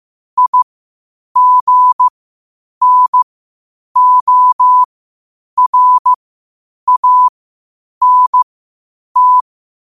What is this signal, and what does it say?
Morse code, heard as a keyed tone, about 15 wpm, spaced as for 8 wpm, 1000 Hz. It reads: IGNORANT
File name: qotdfiles / 8wpm